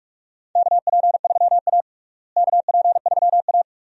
Morse code, heard as a keyed tone, about 45 wpm, 700 Hz. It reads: KP3A KP3A
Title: USA Call Signs - 4 Characters - Repeated 2x - 45wpm